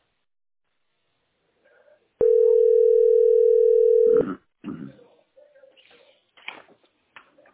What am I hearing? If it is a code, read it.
Ringback tone (North America), 440 + 480 Hz.
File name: 5291aeb3-6831-42dd-95e5